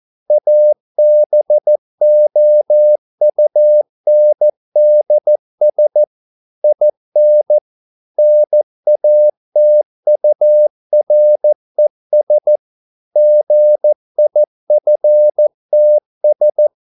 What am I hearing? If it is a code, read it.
Morse code, heard as a keyed tone, about 14 wpm, 600 Hz. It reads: ABOUNDS IN NATURES GIFTS